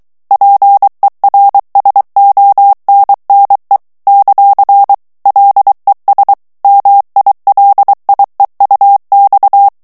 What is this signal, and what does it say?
Morse code, heard as a keyed tone, about 23 wpm, 800 Hz. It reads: PERSONNE ; LES MILIEUX
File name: morse mp3 / FRANCAIS TEXTE CLAIR 2 VITESSE 22 MOTS MINUTES